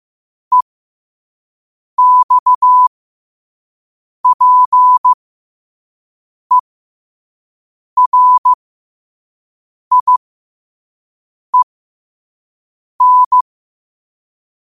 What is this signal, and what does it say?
Morse code, heard as a keyed tone, about 15 wpm, spaced as for 5 wpm, 1000 Hz. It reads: EXPERIEN